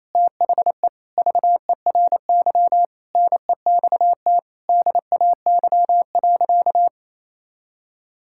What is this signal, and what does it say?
Morse code, heard as a keyed tone, about 28 wpm, 700 Hz. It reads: THE VERY NEXT DAY.
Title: Sentences from Top 1000 Words - Review Part 6 - 28wpm